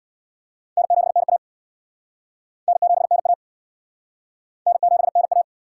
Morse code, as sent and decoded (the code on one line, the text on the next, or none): -. --... -. .- / -. --... -. .- / -. --... -. .-
N7NA N7NA N7NA